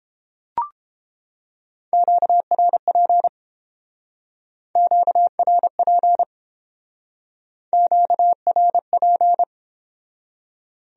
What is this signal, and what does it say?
Morse code, heard as a keyed tone, about 29 wpm, 700 Hz. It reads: QRP QRP QRP